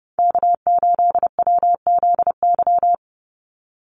Morse code, heard as a keyed tone, about 29 wpm, 700 Hz. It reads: K8WZY
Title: USA Call Signs 30wpm